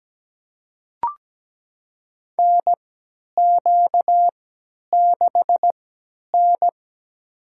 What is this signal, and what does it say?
Morse code, heard as a keyed tone, about 17 wpm, spaced as for 10 wpm, 700 Hz. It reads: NQ6N
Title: USA Call Signs - 3x Character Spacing - 17wpm